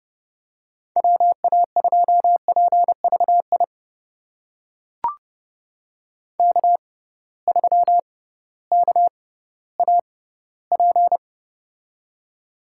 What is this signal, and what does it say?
Morse code, heard as a keyed tone, about 29 wpm, spaced as for 16 wpm, 700 Hz. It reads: WA2PVI K3KAP
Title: USA Call Signs - 6x Character Spacing - 30wpm